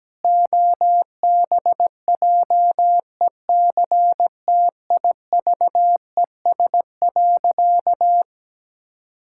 Morse code, heard as a keyed tone, about 17 wpm, 700 Hz. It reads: OBJECTIVES.
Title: Pangram Challenge v2 - 17wpm